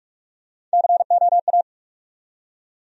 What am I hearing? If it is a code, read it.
Morse code, heard as a keyed tone, about 42 wpm, 700 Hz. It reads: COA
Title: USA Call Sign Postfixes - 3 Letters 45wpm